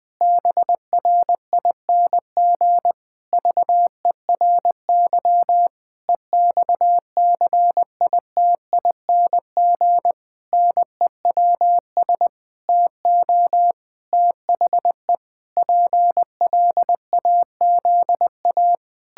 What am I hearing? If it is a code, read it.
Morse code, heard as a keyed tone, about 20 wpm, 700 Hz. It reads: BRING VERY EXCITING NEWS TO THE PLAZA
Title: Pangram Challenge v2 - 20wpm